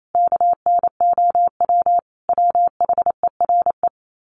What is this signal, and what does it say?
Morse code, heard as a keyed tone, about 28 wpm, 700 Hz. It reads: KNOW WHERE
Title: Sets of 5 Words 28wpm